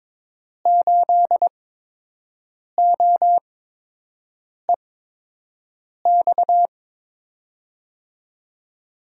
Morse code, heard as a keyed tone, about 22 wpm, spaced as for 6 wpm, 700 Hz. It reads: 8OEX
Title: USA Call Signs - 6 Characters - 8x Character Spacing - 22wpm